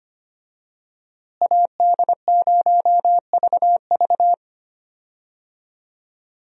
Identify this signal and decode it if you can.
Morse code, heard as a keyed tone, about 25 wpm, 700 Hz. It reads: AD0VV
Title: USA Call Signs - 5 Characters - Repeated 2x - 25wpm